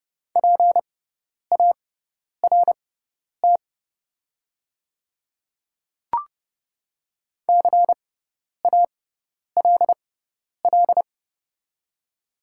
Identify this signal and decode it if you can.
Morse code, heard as a keyed tone, about 30 wpm, spaced as for 10 wpm, 700 Hz. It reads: PART CALL
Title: 6x Character Spacing - Top 200 Words - 30wpm